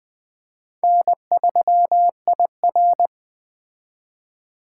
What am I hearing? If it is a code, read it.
Morse code, heard as a keyed tone, about 21 wpm, 700 Hz. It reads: N3IR
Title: Speed Racing - USA 1x2 Call Signs 17wpm